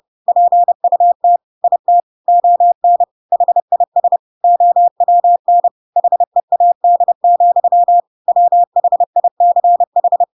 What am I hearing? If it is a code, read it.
Morse code, heard as a keyed tone, about 30 wpm, 700 Hz. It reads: PUT IT ON HIS OWN HEAD, WHICH
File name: Pinocchio / Pinocchio ch3 30 30 ik1hge